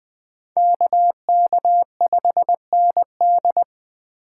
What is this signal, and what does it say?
Morse code, heard as a keyed tone, about 20 wpm, 700 Hz. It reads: KK5ND
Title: USA Call Signs - 5 Characters - 3x Character Spacing - 20wpm